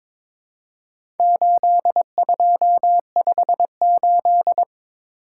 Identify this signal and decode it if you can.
Morse code, heard as a keyed tone, about 22 wpm, 700 Hz. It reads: 8258
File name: practice / 22wpm / Four Letter-Number Combinations - 22wpm